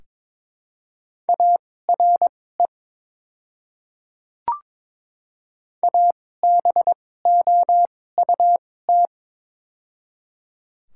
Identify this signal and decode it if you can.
Morse code, heard as a keyed tone, about 22 wpm, spaced as for 16 wpm, 700 Hz. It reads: ARE ABOUT
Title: 2x Character Spacing - Top 200 Words - 22wpm